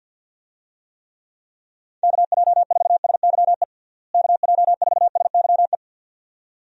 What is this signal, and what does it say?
Morse code, heard as a keyed tone, about 50 wpm, 700 Hz. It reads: KJ4SYE KJ4SYE
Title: USA Call Signs - 6 Characters - Repeated 2x - 50wpm